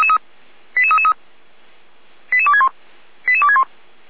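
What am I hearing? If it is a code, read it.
Five-tone selective call (ZVEI), heard as quick runs of five tones, tones of about 70 ms. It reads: 88333 88261 88261
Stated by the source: ZVEI five-tone selective call 88335 received via SDR.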